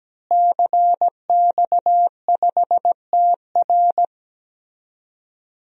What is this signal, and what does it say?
Morse code, heard as a keyed tone, about 17 wpm, 700 Hz. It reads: CX5TR
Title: DXCC-100 Call Signs - 5 Characters - Repeated 3x - 17wpm